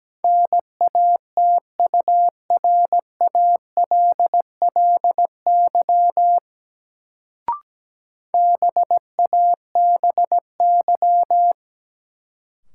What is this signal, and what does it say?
Morse code, heard as a keyed tone, about 17 wpm, 700 Hz. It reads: NATURALLY BABY
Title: Sentences from Top 1300 Words - New Word Drill Alt - 17wpm